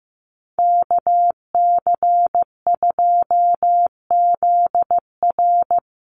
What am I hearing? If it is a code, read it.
Morse code, heard as a keyed tone, about 15 wpm, 700 Hz. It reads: KC2ZR
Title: USA Call Signs 15wpm